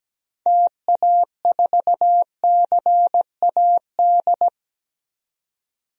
Morse code, heard as a keyed tone, about 17 wpm, 700 Hz. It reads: TA4CAD
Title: DXCC-100 Call Signs - 6 Characters - 17wpm